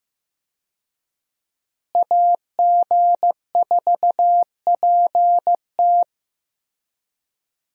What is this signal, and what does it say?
Morse code, heard as a keyed tone, about 16 wpm, 700 Hz. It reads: AG4PT
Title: USA Call Signs - 5 Characters - Repeated 2x - 15wpm